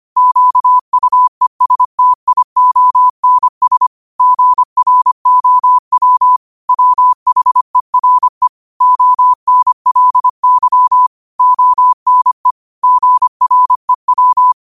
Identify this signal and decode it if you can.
Morse code, heard as a keyed tone, about 25 wpm, 1000 Hz. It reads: QUESTIONS GROW WHERE ONLY ONE GREW